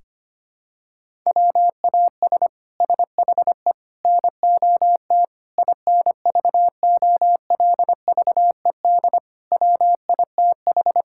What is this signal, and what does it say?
Morse code, heard as a keyed tone, about 25 wpm, 700 Hz. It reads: WAS SHE NOT INVOLVED WITH
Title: Sentences from Top 1000 Words - Review Part 5 - 25wpm